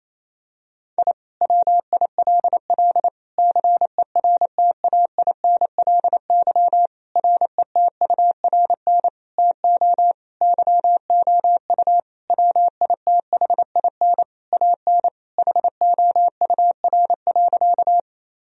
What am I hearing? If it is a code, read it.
Morse code, heard as a keyed tone, about 28 wpm, 700 Hz. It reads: I WILL CERTAINLY RETURN TO YOU WITHIN AN HOUR.
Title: Sentences from Top 1000 Words - Review Part 4 - 28wpm